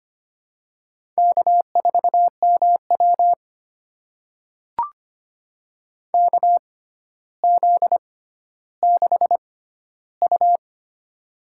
Morse code, heard as a keyed tone, about 25 wpm, spaced as for 13 wpm, 700 Hz. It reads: K4MW KZ6U
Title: USA Call Signs - 4 Characters - 6x Character Spacing - 25wpm